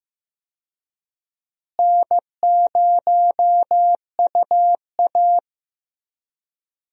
Morse code, heard as a keyed tone, about 15 wpm, 700 Hz. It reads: N0UA
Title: USA Call Signs - 4 Characters - Repeated 3x - 15wpm